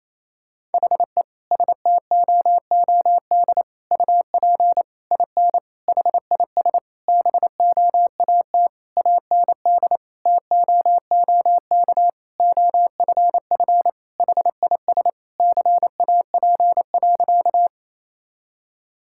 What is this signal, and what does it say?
Morse code, heard as a keyed tone, about 28 wpm, 700 Hz. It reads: HE STOOD UP IN HIS BOAT AND TOOK OFF HIS CAP.
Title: Sentences from Top 1600 Words 28wpm